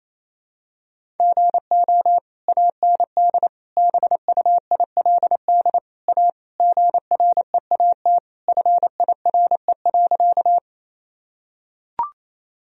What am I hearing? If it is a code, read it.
Morse code, heard as a keyed tone, about 27 wpm, 700 Hz. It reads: GO AND BUILD A GREAT FIRE.
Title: Sentences from Top 1000 Words - Review Part 1 - 28wpm